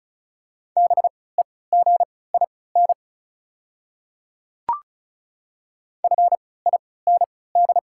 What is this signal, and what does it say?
Morse code, heard as a keyed tone, about 36 wpm, spaced as for 20 wpm, 700 Hz. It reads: BEGIN FIND
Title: 3x Character Spacing - Top 300 Words - 35wpm